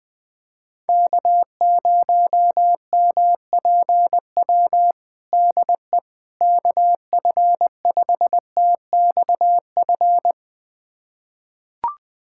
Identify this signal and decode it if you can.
Morse code, heard as a keyed tone, about 20 wpm, 700 Hz. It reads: K0MPW DE KF5TXF
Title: USA Callsign DE Callsign 20wpm